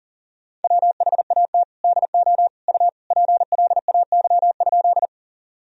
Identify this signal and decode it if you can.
Morse code, heard as a keyed tone, about 37 wpm, 700 Hz. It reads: WHAT DO U PLAY?
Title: Sentences from Top 200 Words and Abbreviations 40wpm